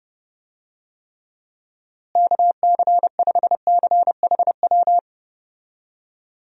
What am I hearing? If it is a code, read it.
Morse code, heard as a keyed tone, about 30 wpm, 700 Hz. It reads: KC5CHW
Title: USA Call Signs - 6 Characters - Repeated 3x - 30wpm